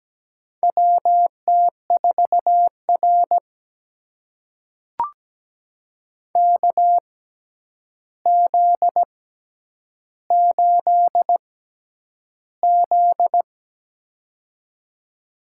Morse code, heard as a keyed tone, about 17 wpm, spaced as for 9 wpm, 700 Hz. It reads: WT4R KZ8Z
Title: USA Call Signs - 4 Characters - 6x Character Spacing - 17wpm